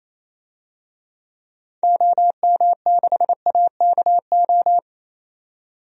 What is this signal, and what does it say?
Morse code, heard as a keyed tone, about 28 wpm, 700 Hz. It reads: OM6AKO
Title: DXCC-100 Call Signs - 6 Characters - 28wpm